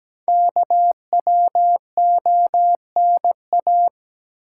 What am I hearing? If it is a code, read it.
Morse code, heard as a keyed tone, about 17 wpm, 700 Hz. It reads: KWONA
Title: Code Groups - 5 Letters - 17wpm